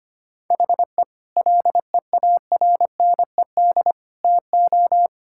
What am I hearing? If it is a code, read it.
Morse code, heard as a keyed tone, about 25 wpm, 700 Hz. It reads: HE LEARNED TO
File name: practice / 25wpm / Difficult Sentences with Hints - 25wpm